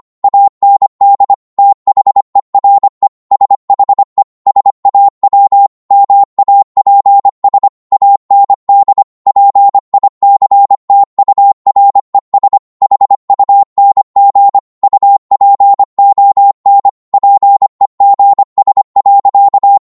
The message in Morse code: AND THERE SHE SAW MAPS AND PICTURES HUNG UPON PEGS.